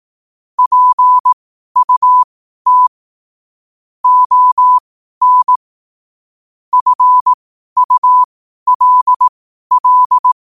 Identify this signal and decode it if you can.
Morse code, heard as a keyed tone, about 18 wpm, spaced as for 12 wpm, 1000 Hz. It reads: PUT ON FULL
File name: 12wpm